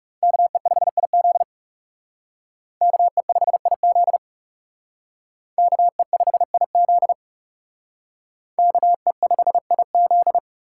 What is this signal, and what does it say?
Morse code, heard as a keyed tone, about 37 wpm, 700 Hz. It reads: KE5IZ KE5IZ KE5IZ KE5IZ
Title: Speed Racing - USA 2x2 Call Signs 30wpm